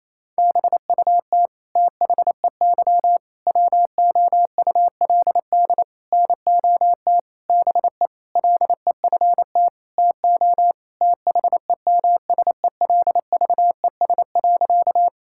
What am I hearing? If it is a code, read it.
Morse code, heard as a keyed tone, about 28 wpm, 700 Hz. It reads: BUT THEY WOULD NOT BE LEFT TO THEMSELVES.
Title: Sentences from Top 1000 Words - Review Part 5 - 28wpm